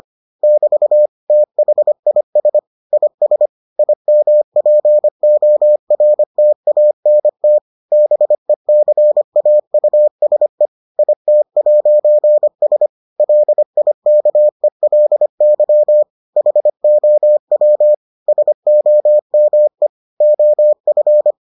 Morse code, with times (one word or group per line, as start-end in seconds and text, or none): 0.4-1.1 <BT>
1.3-2.6 THIS
2.9-3.5 IS
3.8-7.6 IMPORTANT
7.9-10.7 BECAUSE
11.0-12.9 IT'S
13.2-16.0 LIKELY
16.4-17.9 HOW
18.3-19.9 SOME
20.2-21.3 OF